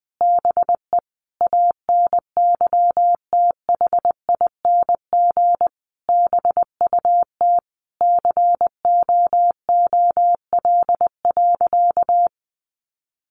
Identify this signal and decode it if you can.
Morse code, heard as a keyed tone, about 20 wpm, 700 Hz. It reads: BE ANYTHING BUT COOL.
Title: Sentences from Top 1200 Words 20wpm